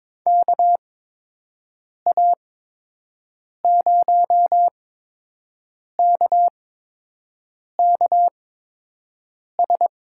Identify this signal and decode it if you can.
Morse code, heard as a keyed tone, about 21 wpm, spaced as for 6 wpm, 700 Hz. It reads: KA0KKS
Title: USA Call Signs - 8x Character Spacing - 22wpm